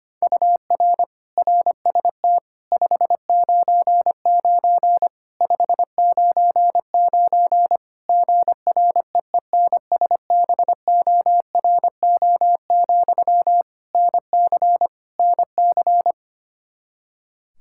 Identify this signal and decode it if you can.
Morse code, heard as a keyed tone, about 25 wpm, 700 Hz. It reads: UR RST 599 599 GREENSBORO, NC NC
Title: K1USN SST QSO Practice 25wpm